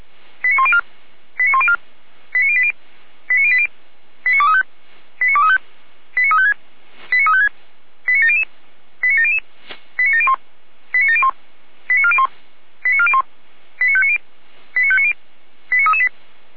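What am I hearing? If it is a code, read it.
Five-tone selective call (ZVEI), heard as quick runs of five tones, tones of about 70 ms. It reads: 89104 89104 89080 89080 89235 89235 89356 89356 89700 89700 89701 89701 89401 89401 89590 89590 89338
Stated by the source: ZVEI five-tone call received on an SDR.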